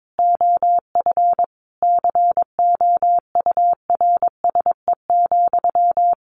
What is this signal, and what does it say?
Morse code, heard as a keyed tone, about 22 wpm, 700 Hz. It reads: OF COURSE,
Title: Sentences from Top 1400 Words 22wpm